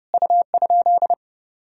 Morse code, heard as a keyed tone, about 32 wpm, 700 Hz. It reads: U?
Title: Sentences from Top 100 Words and Abbreviations 30wpm